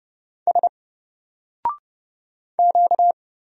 Morse code, heard as a keyed tone, about 29 wpm, 700 Hz. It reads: S Q
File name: Single-Letter-Number-30wpm